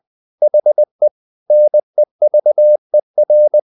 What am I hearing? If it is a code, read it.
Morse code, heard as a keyed tone, about 20 wpm, 600 Hz. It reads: HE NEVER
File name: the old man and the sea / the old man and the sea part 1 20wpm